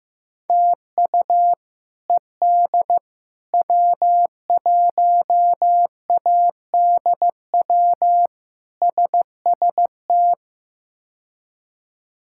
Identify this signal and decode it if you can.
Morse code, heard as a keyed tone, about 15 wpm, 700 Hz. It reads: TU ED W1ADW SST
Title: K1USN SST QSO Practice - 15wpm